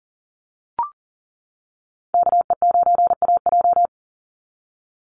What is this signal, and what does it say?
Morse code, heard as a keyed tone, about 40 wpm, 700 Hz. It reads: KE9AJ
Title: NA SOTA Chaser Callsigns 40wpm